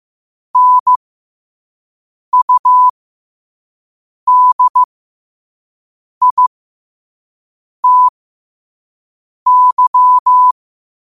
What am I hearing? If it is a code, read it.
Morse code, heard as a keyed tone, about 15 wpm, spaced as for 5 wpm, 1000 Hz. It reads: NUDITY
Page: New quotes every day in morse code at 5 Words per minute.